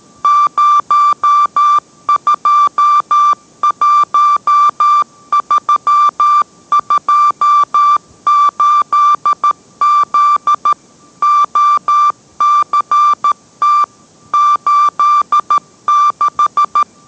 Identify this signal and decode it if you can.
Morse code, heard as a keyed tone, about 13 wpm, 1180 Hz. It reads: 021328Z OCT 86